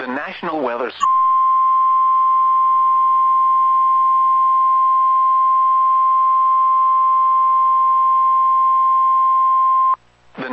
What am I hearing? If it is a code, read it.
Emergency Alert System (SAME) tones (the 1050 Hz attention tone), with speech. It is heard from 1.0 to 10.0 s.